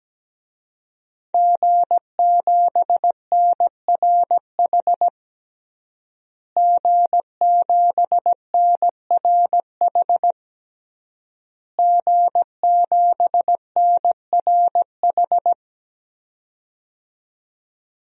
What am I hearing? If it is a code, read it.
Morse code, heard as a keyed tone, about 17 wpm, 700 Hz. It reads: G7NRH G7NRH G7NRH
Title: DXCC-100 Call Signs - 5 Characters - Repeated 3x - 17wpm